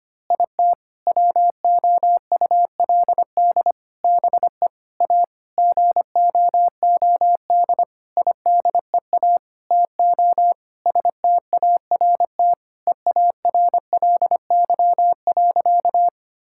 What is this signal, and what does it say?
Morse code, heard as a keyed tone, about 25 wpm, 700 Hz. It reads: IT WOULD BE A GOOD IDEA TO START EARLY.
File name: practice / 25wpm / Sentences with 9-Word Phrases - 25wpm